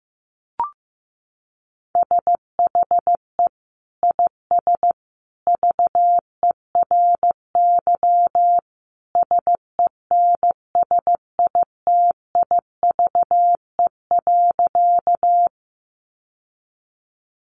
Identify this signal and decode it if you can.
Morse code, heard as a keyed tone, about 15 wpm, 700 Hz. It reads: SHE IS VERY SENSITIVE.
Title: Sentences from Top 1400 Words 15wpm